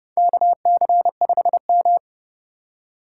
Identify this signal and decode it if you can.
Morse code, heard as a keyed tone, about 30 wpm, 700 Hz. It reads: KC5M
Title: USA Call Signs - 4x Character Spacing - 30wpm